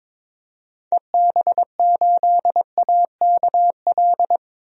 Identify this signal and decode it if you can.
Morse code, heard as a keyed tone, about 21 wpm, 700 Hz. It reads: EB8AKL
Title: DXCC-100 Call Signs - 6 Characters - 22wpm